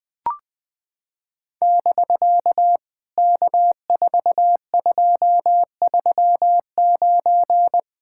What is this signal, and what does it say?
Morse code, heard as a keyed tone, about 20 wpm, 700 Hz. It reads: <BK> K4239
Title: POTA QSO Practice - 20wpm